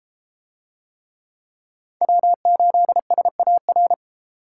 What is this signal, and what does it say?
Morse code, heard as a keyed tone, about 30 wpm, 700 Hz. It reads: W8SAR